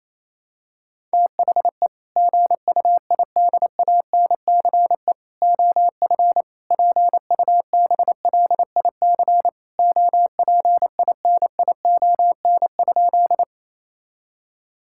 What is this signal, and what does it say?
Morse code, heard as a keyed tone, about 28 wpm, 700 Hz. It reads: THE GUIDANCE OF PUBLIC OPINION?
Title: Sentences from Top 1600 Words 28wpm